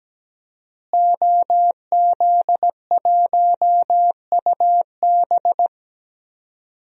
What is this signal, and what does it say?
Morse code, heard as a keyed tone, about 17 wpm, 700 Hz. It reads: OZ1UB
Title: DXCC-100 Call Signs - 5 Characters - Repeated 2x - 17wpm